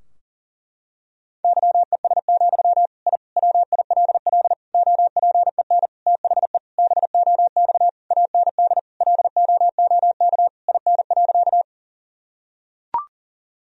Morse code, heard as a keyed tone, about 40 wpm, 700 Hz. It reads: YES, I WILL OPEN THE BOX AND LOOK IN.
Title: Sentences from Top 1000 Words - Review Part 2 - 40wpm